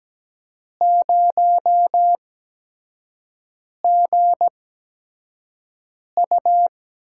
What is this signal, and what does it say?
Morse code, heard as a keyed tone, about 17 wpm, spaced as for 5 wpm, 700 Hz. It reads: 0GU
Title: USA Call Signs - 6 Characters - 8x Character Spacing - 17wpm